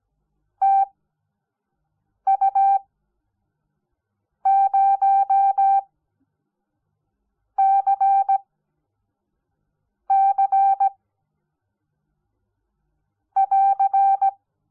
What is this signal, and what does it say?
Morse code, heard as a keyed tone, about 17 wpm, spaced as for 5 wpm, 780 Hz. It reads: TU0CC <AR>
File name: Onlinelehrgang / morsen / DJ4UF Morsen05